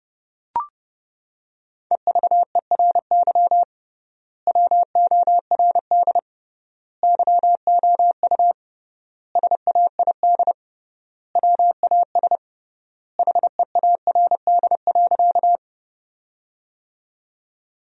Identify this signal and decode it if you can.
Morse code, heard as a keyed tone, about 30 wpm, 700 Hz. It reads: EVERY WORD YOU SAID WAS HEARD.
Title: 3x Word-Spacing - Sentences from Top 400 Words - 30wpm